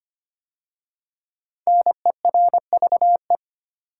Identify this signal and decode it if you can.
Morse code, heard as a keyed tone, about 25 wpm, 700 Hz. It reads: NERVE
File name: 5 Letter Words - First 3 Letters Given - 25wpm